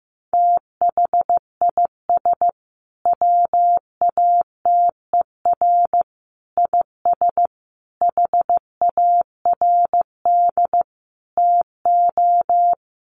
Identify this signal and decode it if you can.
Morse code, heard as a keyed tone, about 15 wpm, 700 Hz. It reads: THIS WATER IS HARD TO
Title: Sentences from Top 1300 Words 15wpm